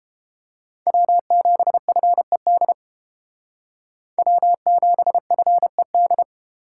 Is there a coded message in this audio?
Yes: Morse code from 0.9 to 6.2 s, heard as a keyed tone, about 30 wpm, 700 Hz. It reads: W7FED W7FED